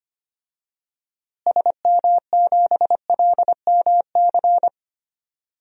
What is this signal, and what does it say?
Morse code, heard as a keyed tone, about 25 wpm, 700 Hz. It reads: SM7LMC
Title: DXCC-100 Call Signs - 6 Characters - Repeated 2x - 25wpm